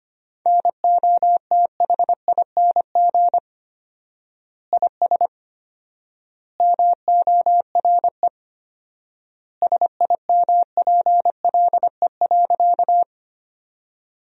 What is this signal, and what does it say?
Morse code, heard as a keyed tone, about 25 wpm, 700 Hz. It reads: NOTHING IS MORE SIMPLE.
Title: 4x Word-Spacing - Sentences from Top 500 Words - 25wpm